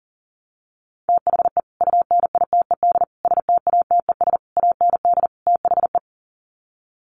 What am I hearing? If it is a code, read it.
Morse code, heard as a keyed tone, about 40 wpm, 700 Hz. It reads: THE UNITED STATES AND THE
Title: Sets of 5 Words 40wpm